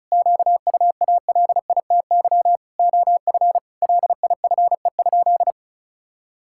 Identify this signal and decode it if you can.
Morse code, heard as a keyed tone, about 34 wpm, 700 Hz. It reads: QUALITY OF LIFE?
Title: Sentences from Top 1000 Words - Review Part 5 - 35wpm